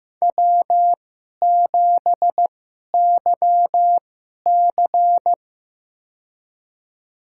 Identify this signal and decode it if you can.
Morse code, heard as a keyed tone, about 15 wpm, spaced as for 11 wpm, 700 Hz. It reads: W7YC
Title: USA Call Signs - 4 Characters - 2x Character Spacing - 15wpm